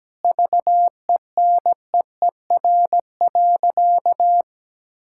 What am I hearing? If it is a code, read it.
Morse code, heard as a keyed tone, about 17 wpm, 700 Hz. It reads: VENEER.
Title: Pangram Challenge v2 - 17wpm